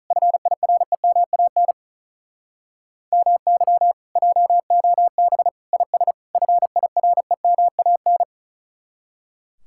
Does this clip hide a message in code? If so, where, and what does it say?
Yes: Morse code from 0.1 to 8.2 s, heard as a keyed tone, about 48 wpm, 700 Hz. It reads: FIREMAN MY JOB IS FIREMAN